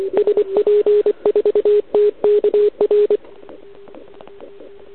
Morse code, heard as a keyed tone, about 25 wpm, 410 Hz. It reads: SP4TKR
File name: ARCHIV / OKDX2019 / OKOM40 KX3